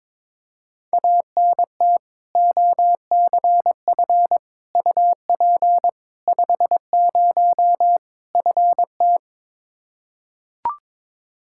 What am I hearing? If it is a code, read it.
Morse code, heard as a keyed tone, about 22 wpm, 700 Hz. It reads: ANT OCF UP 50 FT
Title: Practice QSO - Exchange 2 - Info - 22wpm